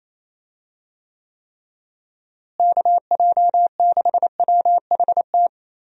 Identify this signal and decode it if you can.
Morse code, heard as a keyed tone, about 27 wpm, 700 Hz. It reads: KJ6WHT
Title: USA Call Signs - 6 Characters - Repeated 3x - 28wpm